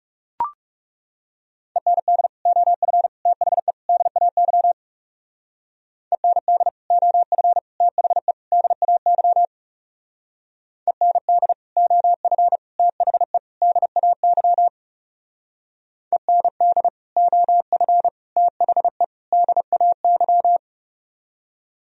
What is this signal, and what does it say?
Morse code, heard as a keyed tone, about 36 wpm, 700 Hz. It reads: END OF THE DAY END OF THE DAY END OF THE DAY END OF THE DAY
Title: Speed-Racing - Sets of 4 Words 30wpm